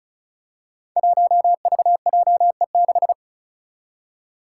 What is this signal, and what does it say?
Morse code, heard as a keyed tone, about 36 wpm, 700 Hz. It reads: 1VJE6
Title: Five Challenging Letter-Number Combinations 35wpm